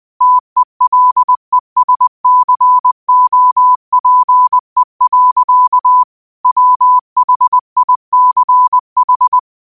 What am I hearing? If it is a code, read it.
Morse code, heard as a keyed tone, about 20 wpm, 1000 Hz. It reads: TELESCOPE. WHICH
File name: qotdfiles / 20wpm